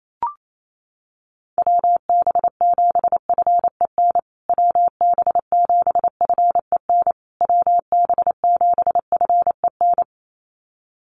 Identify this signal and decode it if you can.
Morse code, heard as a keyed tone, about 28 wpm, 700 Hz. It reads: WB7FEN WB7FEN WB7FEN
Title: USA Call Signs Repeated 3x 28wpm